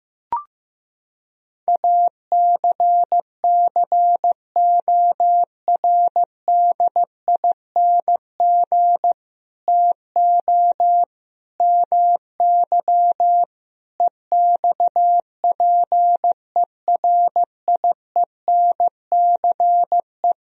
Morse code, heard as a keyed tone, about 15 wpm, 700 Hz. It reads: ACCORDING TO MY EXPERIENCE
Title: Sentences from Top 1000 Words - Review Part 4 - 15wpm